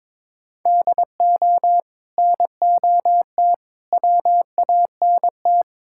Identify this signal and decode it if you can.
Morse code, heard as a keyed tone, about 22 wpm, 700 Hz. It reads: DO NOT WANT
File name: practice / 22wpm / Build A Phrase - 5-Word Phrases - 22wpm